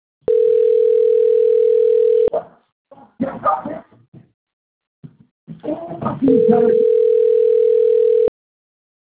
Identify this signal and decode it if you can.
Ringback tone (North America), 440 + 480 Hz, with speech.
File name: rg-1600-unknown- - - .14558